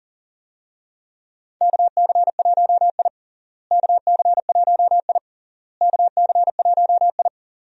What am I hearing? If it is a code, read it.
Morse code, heard as a keyed tone, about 37 wpm, 700 Hz. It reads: KC1I KC1I KC1I
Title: USA Call Signs - 4 Characters - Repeated 3x - 40wpm